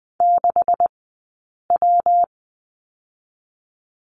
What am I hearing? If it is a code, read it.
Morse code, heard as a keyed tone, about 20 wpm, spaced as for 8 wpm, 700 Hz. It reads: 6W
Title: 126 Lesson 21d - Number 6 - Two Characters 20wpm